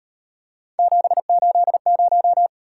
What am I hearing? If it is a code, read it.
Morse code, heard as a keyed tone, about 37 wpm, 700 Hz. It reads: 780